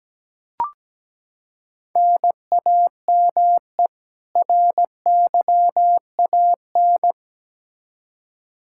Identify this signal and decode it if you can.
Morse code, heard as a keyed tone, about 17 wpm, 700 Hz. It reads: NAME RYAN
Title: K1USN SST QSO Practice 17wpm